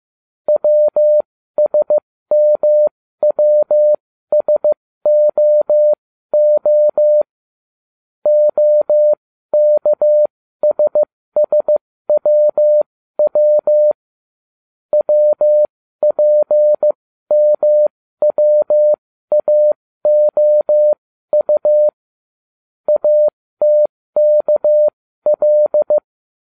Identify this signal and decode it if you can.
Morse code, heard as a keyed tone, about 15 wpm, spaced as for 13 wpm, 600 Hz. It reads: WSMWSOO OKSSWW WPMWAOU ATKL